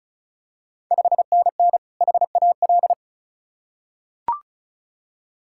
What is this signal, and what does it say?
Morse code, heard as a keyed tone, about 36 wpm, 690 Hz. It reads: 5NN HAL
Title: Ontario QSO Party Exchange 35wpm